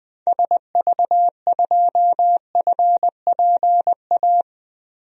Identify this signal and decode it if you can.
Morse code, heard as a keyed tone, about 20 wpm, 700 Hz. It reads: SV2FPA
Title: DXCC-100 Call Signs - 6 Characters - Repeated 3x - 20wpm